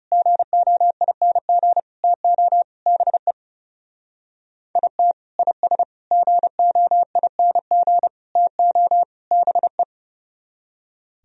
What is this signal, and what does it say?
Morse code, heard as a keyed tone, about 38 wpm, 700 Hz. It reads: GOING TO BE IT IS GOING TO BE